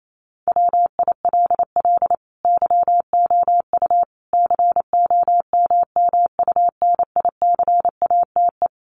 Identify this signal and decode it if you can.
Morse code, heard as a keyed tone, about 28 wpm, 700 Hz. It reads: WILL YOU COMMUNICATE
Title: Sentences from Top 1300 Words 28wpm